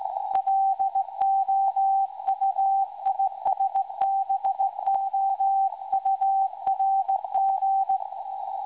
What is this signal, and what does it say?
Morse code, heard as a keyed tone, about 17 wpm, 780 Hz. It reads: LOUISBOURG
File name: WebSDR records / SAQ recording 17.2kHz